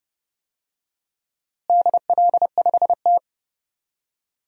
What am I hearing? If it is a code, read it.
Morse code, heard as a keyed tone, about 30 wpm, 700 Hz. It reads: DL5T
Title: DXCC-100 Call Signs - 4 Characters - Repeated 3x - 30wpm